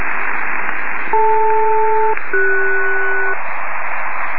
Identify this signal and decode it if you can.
Aircraft SELCAL, heard as two pairs of tones. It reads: DL-CS
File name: 66web / hfselcal